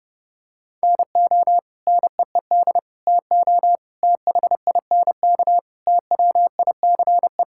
Morse code, heard as a keyed tone, about 30 wpm, 700 Hz. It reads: NO NEED TO THINK TWICE
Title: Sentences from Top 1600 Words 30wpm